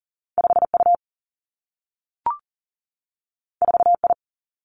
Morse code, heard as a keyed tone, about 39 wpm, 700 Hz. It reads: 5 U 4 I
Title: Two Challenging Letter-Number Combinations 40wpm